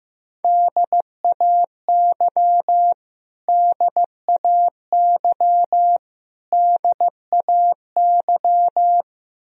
Morse code, heard as a keyed tone, about 15 wpm, 700 Hz. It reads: DAY DAY DAY
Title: Top 100 Words Repeated 15wpm